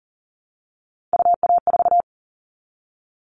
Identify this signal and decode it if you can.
Morse code, heard as a keyed tone, about 40 wpm, 700 Hz. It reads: UA4
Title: All Cut Numbers - 3-Digit 40wpm